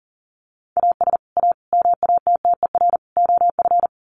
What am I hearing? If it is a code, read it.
Morse code, heard as a keyed tone, about 40 wpm, 700 Hz. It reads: AS A MATTER OF
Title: Sets of 4 Words 40wpm